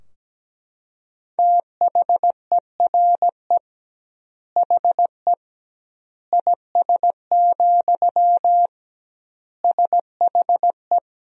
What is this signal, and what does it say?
Morse code, heard as a keyed tone, about 17 wpm, 700 Hz. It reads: THERE HE IS, SHE
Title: 2x Word-Spacing - Sentences from Top 100 Words - 17wpm